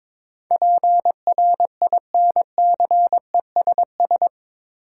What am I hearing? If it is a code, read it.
Morse code, heard as a keyed tone, about 22 wpm, 700 Hz. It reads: PRINCESS
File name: practice / 22wpm / Eight Letter Words - 22wpm